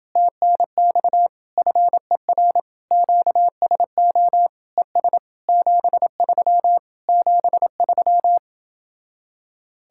Morse code, heard as a keyed tone, about 30 wpm, 700 Hz. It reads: TNX FER QSO ES 73 73